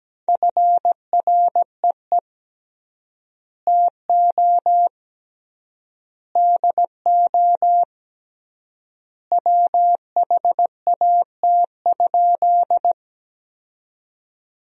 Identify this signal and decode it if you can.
Morse code, heard as a keyed tone, about 17 wpm, 700 Hz. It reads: FREE TO DO WHAT?
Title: 3x Word-Spacing - Sentences from Top 500 Words - 17wpm